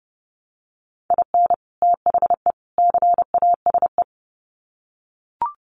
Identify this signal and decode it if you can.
Morse code, heard as a keyed tone, about 30 wpm, 700 Hz. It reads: IN THE CASE
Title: Sets of 3 Words 30wpm